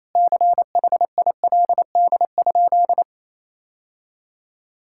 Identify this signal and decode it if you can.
Morse code, heard as a keyed tone, about 28 wpm, 700 Hz. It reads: CHILD?
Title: Sentences from Top 1600 Words 28wpm